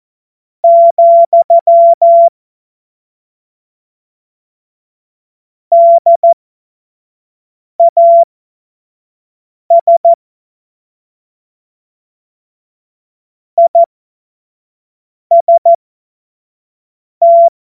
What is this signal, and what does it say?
Morse code, heard as a keyed tone, about 14 wpm, spaced as for 5 wpm, 680 Hz. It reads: , DAS IST
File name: Morsekurs / Witze / Witz 15. Juni 2021, 25 F70